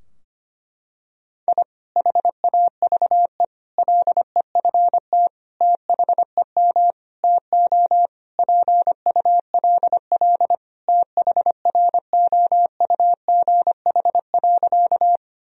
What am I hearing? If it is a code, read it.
Morse code, heard as a keyed tone, about 25 wpm, 700 Hz. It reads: I HAVE LEFT THEM TO PULL THROUGH.
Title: Sentences from Top 1000 Words - Review Part 2 - 25wpm